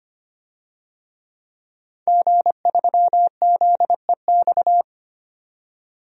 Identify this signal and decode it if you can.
Morse code, heard as a keyed tone, about 25 wpm, 700 Hz. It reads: G3ZEX
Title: DXCC-100 Call Signs - 5 Characters - 25wpm